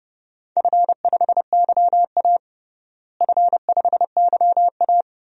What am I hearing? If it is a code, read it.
Morse code, heard as a keyed tone, about 30 wpm, 700 Hz. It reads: F5YA F5YA
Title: DXCC-100 Call Signs - 4 Characters - Repeated 2x - 30wpm